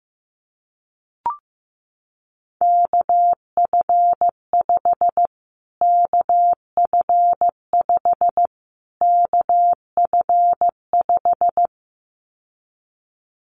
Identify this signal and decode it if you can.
Morse code, heard as a keyed tone, about 15 wpm, 700 Hz. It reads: KF5 KF5 KF5
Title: Callsign Prefixes for Top 25 DXCC - Repeated 3x - 15wpm